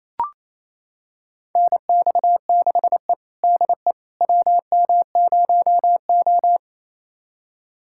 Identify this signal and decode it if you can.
Morse code, heard as a keyed tone, about 28 wpm, 700 Hz. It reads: NX6E DE WM0O
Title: USA Callsign DE Callsign - 4 Characters - 28wpm